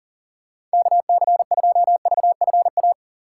Morse code, heard as a keyed tone, about 40 wpm, 700 Hz. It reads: KC2VFA
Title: USA Call Signs - 6 Characters - 3x Character Spacing - 40wpm